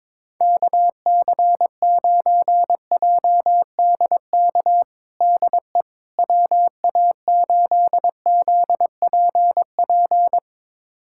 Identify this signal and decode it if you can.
Morse code, heard as a keyed tone, about 22 wpm, 700 Hz. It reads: KC9JDK DE WA8ZPP
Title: USA Callsign DE Callsign - 6 Characters - 22wpm